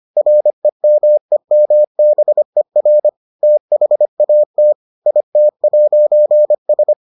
Morse code, heard as a keyed tone, about 25 wpm, 600 Hz. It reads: REMEMBER THAT IT'S